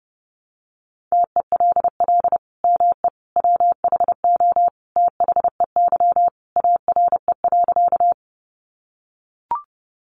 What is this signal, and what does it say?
Morse code, heard as a keyed tone, about 28 wpm, 700 Hz. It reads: TELL ME WHO THEY ARE.
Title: Sentences from Top 700 Words 30wpm